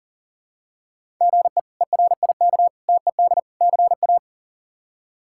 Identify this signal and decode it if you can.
Morse code, heard as a keyed tone, about 40 wpm, 700 Hz. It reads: GE ERIK TED CA
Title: K1USN SST QSO Practice - 40wpm